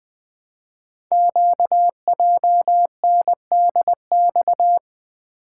Morse code, heard as a keyed tone, about 20 wpm, 700 Hz. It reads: QJNDX
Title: Code Groups - 5 Letters - 20wpm